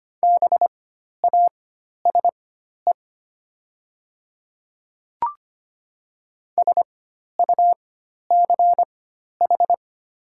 Morse code, heard as a keyed tone, about 25 wpm, spaced as for 12 wpm, 700 Hz. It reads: BASE SUCH
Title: 4x Character Spacing - Top 500 Words - 25wpm